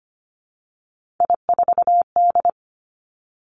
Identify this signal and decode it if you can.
Morse code, heard as a keyed tone, about 25 wpm, 700 Hz. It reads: I4D
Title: Three Challenging Letter-Number Combinations 25wpm